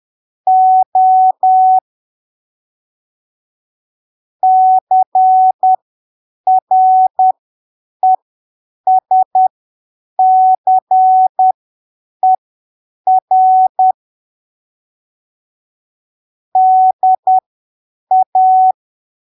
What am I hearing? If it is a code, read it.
Morse code, heard as a keyed tone, about 10 wpm, spaced as for 7 wpm, 750 Hz. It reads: O CRESCER DA
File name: arquivos / texto / texto 29 10ppm